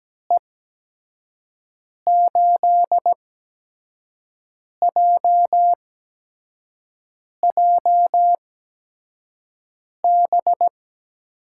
Morse code, heard as a keyed tone, about 17 wpm, spaced as for 5 wpm, 700 Hz. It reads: E8JJB
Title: USA Call Signs - 6 Characters - 8x Character Spacing - 17wpm